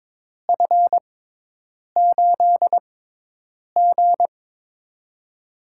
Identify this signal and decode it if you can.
Morse code, heard as a keyed tone, about 23 wpm, spaced as for 8 wpm, 700 Hz. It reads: F8G
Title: USA Call Signs - 4 Characters - 6x Character Spacing - 22wpm